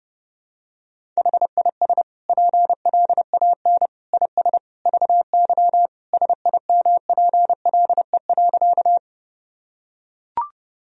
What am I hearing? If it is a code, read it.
Morse code, heard as a keyed tone, about 32 wpm, 700 Hz. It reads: HIS PLAN IS VY SIMPLE.
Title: Sentences from Top 500 Words and Abbreviations 30wpm